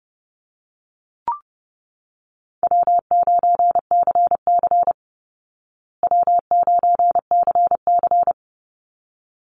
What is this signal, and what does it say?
Morse code, heard as a keyed tone, about 30 wpm, 700 Hz. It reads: W9CC W9CC
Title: Easy Call Signs Repeated 2x - 30wpm